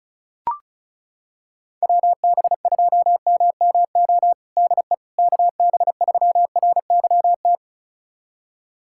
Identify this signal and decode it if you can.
Morse code, heard as a keyed tone, about 36 wpm, 690 Hz. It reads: WB2MMO DE KB3RYT
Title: USA Callsign DE Callsign - 6 Characters - 35wpm